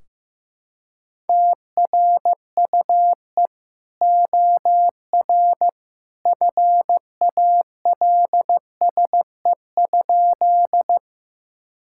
Morse code, heard as a keyed tone, about 15 wpm, 700 Hz. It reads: TRUE OR FALSE?
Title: Sentences from Top 1600 Words 15wpm